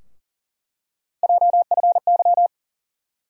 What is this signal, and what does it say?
Morse code, heard as a keyed tone, about 40 wpm, 700 Hz. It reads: JFY
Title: USA Call Sign Postfixes - 3 Letters 40wpm